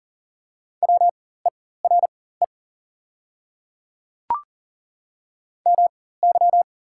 Morse code, heard as a keyed tone, about 40 wpm, spaced as for 19 wpm, 700 Hz. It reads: WERE MY
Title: 4x Character Spacing - Top 100 Words - 40wpm